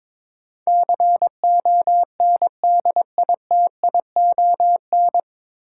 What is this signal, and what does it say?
Morse code, heard as a keyed tone, about 22 wpm, 700 Hz. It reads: CONDITION
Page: Morse Code Ninja practice 22wpm